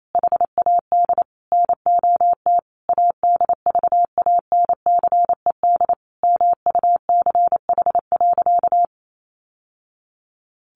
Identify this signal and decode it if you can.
Morse code, heard as a keyed tone, about 28 wpm, 700 Hz. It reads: HAD NOT ADVANCED MUCH.
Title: Sentences from Top 1200 Words 28wpm